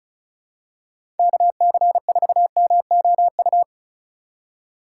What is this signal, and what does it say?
Morse code, heard as a keyed tone, about 36 wpm, 700 Hz. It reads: KC4MOU
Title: USA Call Signs - 3x Character Spacing - 35wpm